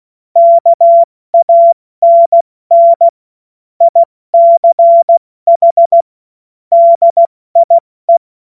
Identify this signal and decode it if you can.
Morse code, heard as a keyed tone, about 16 wpm, 680 Hz. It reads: KANN ICH DIE
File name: Morsekurs / Witze / Witz 24 Juni Mai 2025 70 F80